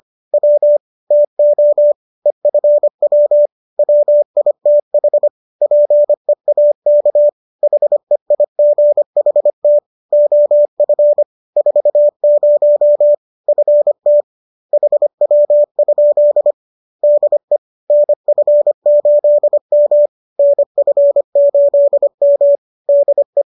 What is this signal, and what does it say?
Morse code, heard as a keyed tone, about 25 wpm, 590 Hz. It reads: W TO EFW WITH PEAK HEIGHT OF 40 FT HW? DE NF8M NF8M DE
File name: qso221 25